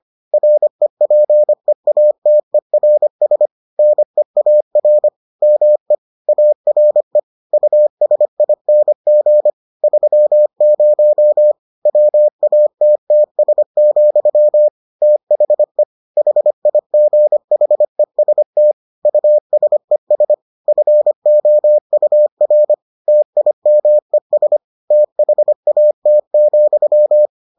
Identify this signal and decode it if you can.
Morse code, heard as a keyed tone, about 25 wpm, 600 Hz. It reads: REPEATERS NEAR ME ARE USING 30 WATTS, THE HIGHEST USES FOUR TIMES THAT,